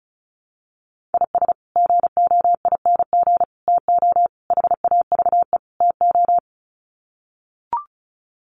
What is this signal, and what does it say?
Morse code, heard as a keyed tone, about 34 wpm, 700 Hz. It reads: IS GOING TO HAVE TO
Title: Sets of 5 Words 35wpm